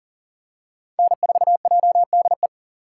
Morse code, heard as a keyed tone, about 39 wpm, 700 Hz. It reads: N4JDE
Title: USA Call Signs - 5 Characters - 2x Character Spacing - 40wpm